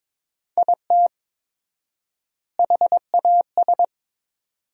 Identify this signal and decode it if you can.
Morse code, heard as a keyed tone, about 22 wpm, 700 Hz. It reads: IT HAS
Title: 4x Word-Spacing - Sentences from Top 500 Words - 22wpm